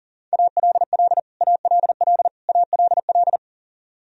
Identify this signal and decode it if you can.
Morse code, heard as a keyed tone, about 40 wpm, 700 Hz. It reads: ALL ALL ALL
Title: Top 100 Words Repeated 40wpm